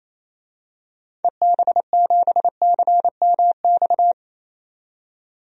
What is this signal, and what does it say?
Morse code, heard as a keyed tone, about 28 wpm, 700 Hz. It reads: EB7CMX
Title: DXCC-100 Call Signs - 6 Characters - 28wpm